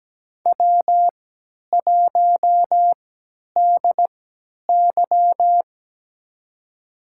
Morse code, heard as a keyed tone, about 17 wpm, spaced as for 10 wpm, 700 Hz. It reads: W1DY
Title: USA Call Signs - 4 Characters - 3x Character Spacing - 17wpm